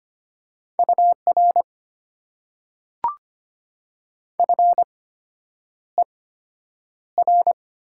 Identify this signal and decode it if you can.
Morse code, heard as a keyed tone, about 25 wpm, spaced as for 9 wpm, 700 Hz. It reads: UR FER
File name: Top 90 QSO Elements - 8x Character Spacing - 25wpm